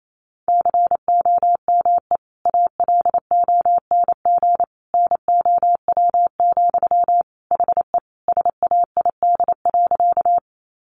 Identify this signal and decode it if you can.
Morse code, heard as a keyed tone, about 28 wpm, 700 Hz. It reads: COME ALONG NOW, HE SAID.
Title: Sentences from Top 700 Words 28wpm